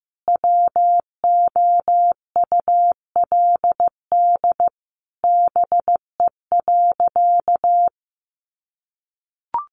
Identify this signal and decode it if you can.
Morse code, heard as a keyed tone, about 15 wpm, 700 Hz. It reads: WOULD BE.
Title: Sentences from Top 200 Words 15wpm